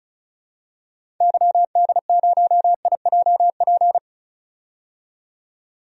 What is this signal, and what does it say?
Morse code, heard as a keyed tone, about 34 wpm, 700 Hz. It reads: YD0IJP
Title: DXCC-100 Call Signs - 6 Characters - 35wpm